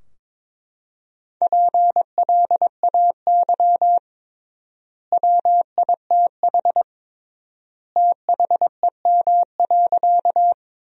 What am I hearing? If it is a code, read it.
Morse code, heard as a keyed tone, about 22 wpm, 700 Hz. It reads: PLAY WITH THEM.
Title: 3x Word-Spacing - Sentences from Top 200 Words - 22wpm